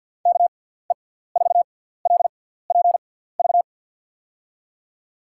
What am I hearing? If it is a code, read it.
Morse code, heard as a keyed tone, about 47 wpm, spaced as for 17 wpm, 700 Hz. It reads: KE4LPV
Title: USA Call Signs - 6 Characters - 6x Character Spacing - 50wpm